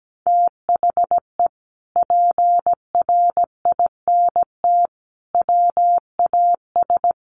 Morse code, heard as a keyed tone, about 17 wpm, 700 Hz. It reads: THE PRINT WAS
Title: Sentences from Top 1400 Words 17wpm